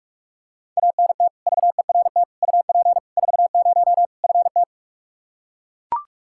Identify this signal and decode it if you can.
Morse code, heard as a keyed tone, about 41 wpm, 700 Hz. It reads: ANT VERT UP 40 FT
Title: K1USN SST QSO Practice 45wpm